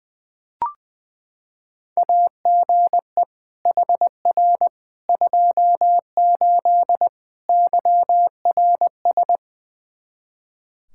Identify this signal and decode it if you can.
Morse code, heard as a keyed tone, about 20 wpm, 700 Hz. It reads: AGE HR 28 YRS
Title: Practice QSO - Exchange 3 - More Info - 20wpm